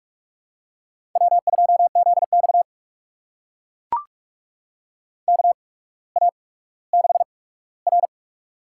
Morse code, heard as a keyed tone, about 44 wpm, spaced as for 19 wpm, 700 Hz. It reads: W2ZX KA6R
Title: USA Call Signs - 4 Characters - 8x Character Spacing - 45wpm